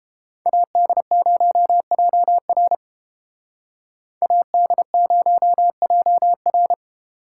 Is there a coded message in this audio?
Yes: Morse code from 0.5 to 6.7 s, heard as a keyed tone, about 31 wpm, 700 Hz. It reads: AD0JR AD0JR